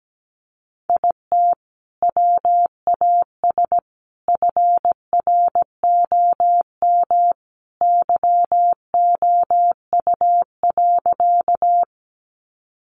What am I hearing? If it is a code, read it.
Morse code, heard as a keyed tone, about 17 wpm, 700 Hz. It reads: IT WAS FROM YOU.
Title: Sentences from Top 100 Words 17wpm